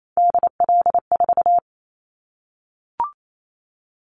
Morse code, heard as a keyed tone, about 28 wpm, 700 Hz. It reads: DL4
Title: Callsign Prefixes for Top 50 DXCC - 28wpm